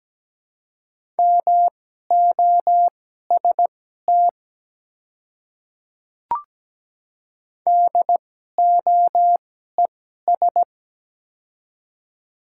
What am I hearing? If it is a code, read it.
Morse code, heard as a keyed tone, about 17 wpm, spaced as for 12 wpm, 700 Hz. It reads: MOST DOES
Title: 2x Character Spacing - Top 200 Words - 17wpm